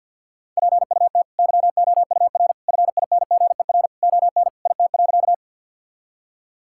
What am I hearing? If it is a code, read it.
Morse code, heard as a keyed tone, about 50 wpm, 700 Hz. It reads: PUT YOUR FINGER ON IT.
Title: Sentences from Top 1600 Words 50wpm